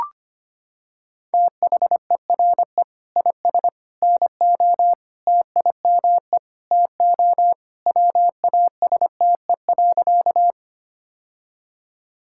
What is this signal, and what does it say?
Morse code, heard as a keyed tone, about 25 wpm, 700 Hz. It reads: THERE IS NO TIME TO WASTE.
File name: practice / 25wpm / Sentences with 3-Word Phrases - 25wpm